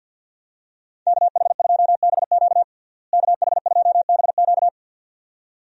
Morse code, heard as a keyed tone, about 51 wpm, 700 Hz. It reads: KH2BQ KH2BQ
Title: USA Call Signs - 5 Characters - Repeated 2x - 50wpm